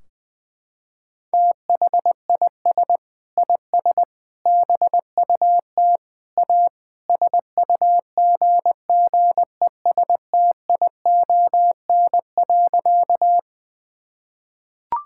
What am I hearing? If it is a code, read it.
Morse code, heard as a keyed tone, about 21 wpm, 700 Hz. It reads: THIS IS BUT A SUGGESTION.
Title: Sentences from Top 1600 Words 20wpm